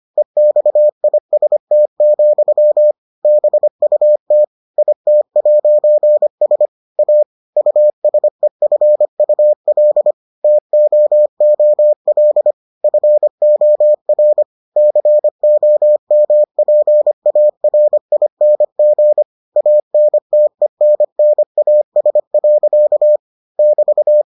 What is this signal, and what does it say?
Morse code, heard as a keyed tone, about 25 wpm, 600 Hz. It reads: EXIST, BUT IT'S A USEFUL TOOL FOR COMPARING ANTENNAS. <BT>